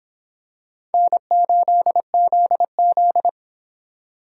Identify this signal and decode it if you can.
Morse code, heard as a keyed tone, about 26 wpm, 700 Hz. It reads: N8ZZ